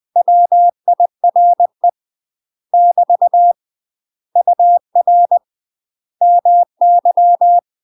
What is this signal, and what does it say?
Morse code, heard as a keyed tone, about 20 wpm, 700 Hz. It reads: WIRE <BT> UR MY
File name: UK MASTER QSOs TRAINING / 20WPM / UK MASTER QSOs 0043